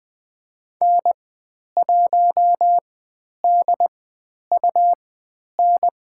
Morse code, heard as a keyed tone, about 20 wpm, spaced as for 10 wpm, 700 Hz. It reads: N1DUN
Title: 102 Lesson 17g - Letter u - Call Signs 20wpm